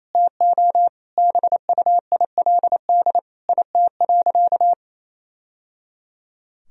Morse code, heard as a keyed tone, about 28 wpm, 700 Hz. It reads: TO BUILD IT.
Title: Sentences from Top 1000 Words - Review Part 1 - 28wpm